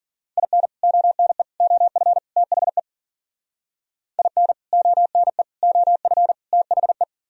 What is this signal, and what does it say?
Morse code, heard as a keyed tone, about 50 wpm, 700 Hz. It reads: IN ONE OF THE IN ONE OF THE